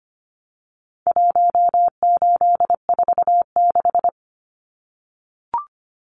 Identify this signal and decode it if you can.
Morse code, heard as a keyed tone, about 25 wpm, 700 Hz. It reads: 1846
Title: Numbers - Contest Style - 4-Digit 25wpm